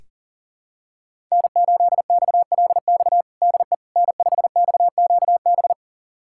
Morse code, heard as a keyed tone, about 40 wpm, 700 Hz. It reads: N8XLX DE N5XQB
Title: USA Callsign DE Callsign - 5 Characters - 40wpm